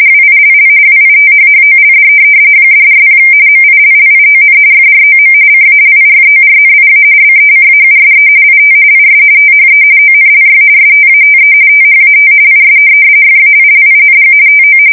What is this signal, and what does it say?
RTTY, about 45.45 baud. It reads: WILL WITHHOLD $40 MILLION FROM CALIFORNIA BECAUSE IT IS THE ONLY STATE THAT IS FAILING TO
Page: 170 Shift 2125/2295